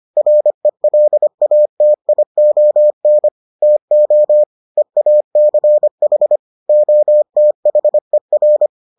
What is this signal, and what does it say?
Morse code, heard as a keyed tone, about 25 wpm, 600 Hz. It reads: RELATION TO EACH OTHER